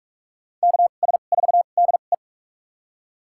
Morse code, heard as a keyed tone, about 47 wpm, spaced as for 33 wpm, 700 Hz. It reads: KS4DE
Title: USA Call Signs - 5 Characters - 2x Character Spacing - 45wpm